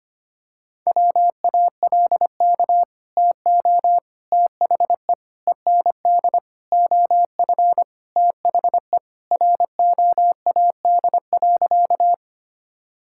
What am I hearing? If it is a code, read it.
Morse code, heard as a keyed tone, about 25 wpm, 700 Hz. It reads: WALK TO THE END OF THE ROAD.
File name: practice / 25wpm / Sentences with 4-Word Phrases - 25wpm